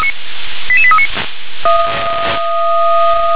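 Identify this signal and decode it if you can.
Five-tone selective call (ZVEI), heard as quick runs of five tones, tones of about 70 ms. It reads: 88839
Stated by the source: ZVEI five-tone call received on an SDR.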